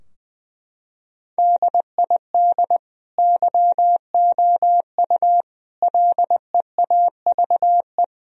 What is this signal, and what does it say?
Morse code, heard as a keyed tone, about 20 wpm, 700 Hz. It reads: DID YOU LEAVE
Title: Sentences from Top 1000 Words - Review Part 4 - 20wpm